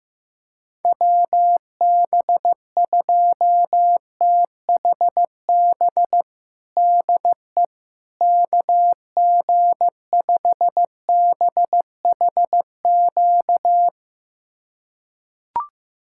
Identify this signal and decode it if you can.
Morse code, heard as a keyed tone, about 15 wpm, 700 Hz. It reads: WB2THB DE KG5BHQ
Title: USA Callsign DE Callsign 15wpm